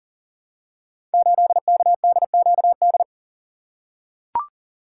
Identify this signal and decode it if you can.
Morse code, heard as a keyed tone, about 42 wpm, 700 Hz. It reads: 8KDQD
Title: Code Groups - 5 Letter:Numbers - 40wpm